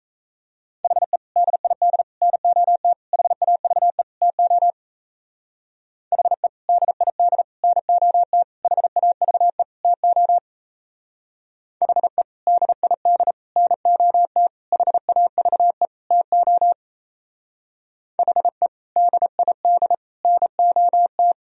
Morse code, heard as a keyed tone, about 33 wpm, 700 Hz. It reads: HE DID NOT HAVE TO HE DID NOT HAVE TO HE DID NOT HAVE TO HE DID NOT
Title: Speed-Racing - Sets of 5 Words 28wpm